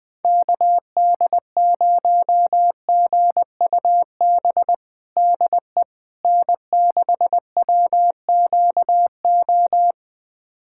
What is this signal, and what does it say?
Morse code, heard as a keyed tone, about 20 wpm, 700 Hz. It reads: KD0GUB DE N6WQO
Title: USA Callsign DE Callsign 20wpm